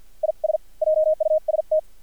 Morse code, heard as a keyed tone, about 50 wpm, spaced as for 43 wpm, 620 Hz. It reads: IS JUST
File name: archived / CRSnet